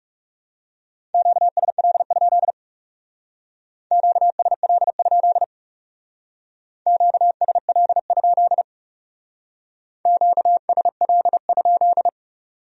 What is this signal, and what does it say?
Morse code, heard as a keyed tone, about 38 wpm, 700 Hz. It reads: QSL? QSL? QSL? QSL?